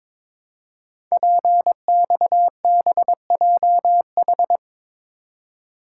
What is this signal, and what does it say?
Morse code, heard as a keyed tone, about 23 wpm, 700 Hz. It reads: PXBJH
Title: Code Groups - 5 Letters - 22wpm